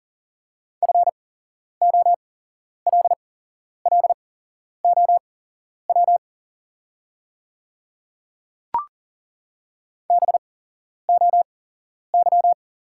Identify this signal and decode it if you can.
Morse code, heard as a keyed tone, about 40 wpm, spaced as for 11 wpm, 700 Hz. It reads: FOLLOW BOY
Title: 8x Character Spacing - Top 200 Words - 40wpm